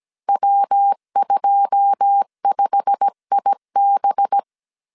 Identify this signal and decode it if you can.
Morse code, heard as a keyed tone, about 17 wpm, 800 Hz. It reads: W25IB
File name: CW / DET / 1 9 1500C S C 16wpm 2909